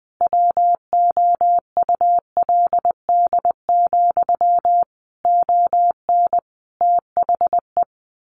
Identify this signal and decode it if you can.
Morse code, heard as a keyed tone, about 20 wpm, 700 Hz. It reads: WOULD, ON THE
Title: A Sound Of Thunder 20wpm